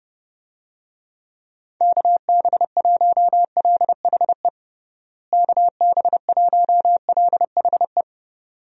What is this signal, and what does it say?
Morse code, heard as a keyed tone, about 30 wpm, 700 Hz. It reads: KB1LHE KB1LHE
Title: USA Call Signs - 6 Characters - Repeated 2x - 30wpm